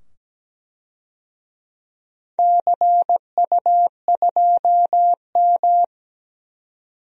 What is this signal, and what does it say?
Morse code, heard as a keyed tone, about 17 wpm, 700 Hz. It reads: CU2M
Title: DXCC-100 Call Signs - 4 Characters - 17wpm